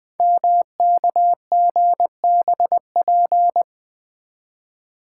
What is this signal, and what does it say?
Morse code, heard as a keyed tone, about 21 wpm, 700 Hz. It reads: MKGBP
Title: Code Groups - 5 Letters - 20wpm